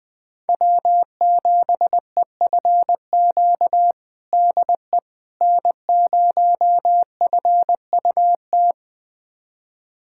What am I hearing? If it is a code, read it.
Morse code, heard as a keyed tone, about 21 wpm, 700 Hz. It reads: W7EFQ DE N0FUT
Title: USA Callsign DE Callsign - 5 Characters - 20wpm